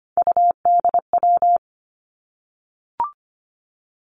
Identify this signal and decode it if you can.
Morse code, heard as a keyed tone, about 25 wpm, 700 Hz. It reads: UDW
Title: Three Challenging Letter-Number Combinations 25wpm